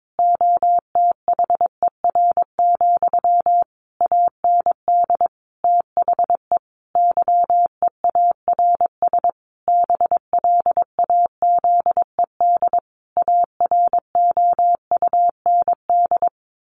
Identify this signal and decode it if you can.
Morse code, heard as a keyed tone, about 22 wpm, 700 Hz. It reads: OTHER, AND THE YEARS BLAZED AROUND
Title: A Sound Of Thunder 22wpm